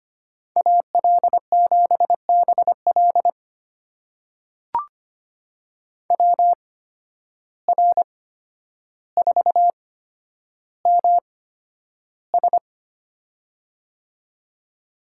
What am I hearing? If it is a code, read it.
Morse code, heard as a keyed tone, about 25 wpm, spaced as for 11 wpm, 700 Hz. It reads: AL7BL WR4MS
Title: USA Call Signs - 5 Characters - 8x Character Spacing - 25wpm